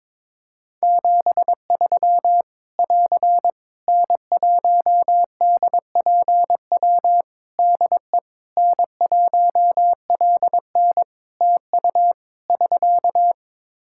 Morse code, heard as a keyed tone, about 22 wpm, 700 Hz. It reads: 73 <AR> N1DPW DE N1LN TU <SK>
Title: Practice QSO - Exchange 4 - Ending - 22wpm